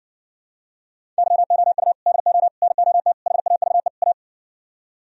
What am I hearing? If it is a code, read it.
Morse code, heard as a keyed tone, about 59 wpm, 700 Hz. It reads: YOU DO NOT HAVE A